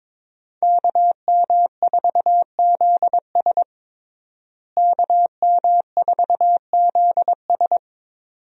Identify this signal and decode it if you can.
Morse code, heard as a keyed tone, about 22 wpm, 700 Hz. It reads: KM4ZS KM4ZS
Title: USA Call Signs - 5 Characters - Repeated 2x - 22wpm